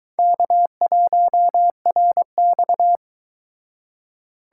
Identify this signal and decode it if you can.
Morse code, heard as a keyed tone, about 22 wpm, 700 Hz. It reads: K1RX